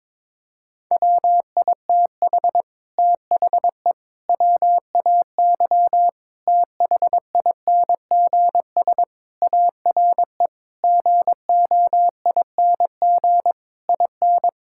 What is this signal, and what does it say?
Morse code, heard as a keyed tone, about 22 wpm, 700 Hz. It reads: WITH THE WAY THINGS ARE GOING IN
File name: practice / 22wpm / Sentences with 8-Word Phrases - 22wpm